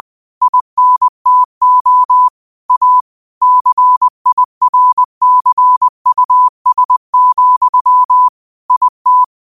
Morse code, heard as a keyed tone, about 20 wpm, 1000 Hz. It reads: INTO A CIRCUS, IT
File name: qotdfiles / 20wpm